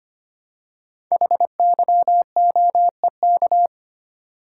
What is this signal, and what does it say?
Morse code, heard as a keyed tone, about 25 wpm, 700 Hz. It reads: HYOEK
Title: Code Groups - 5 Letters - 25wpm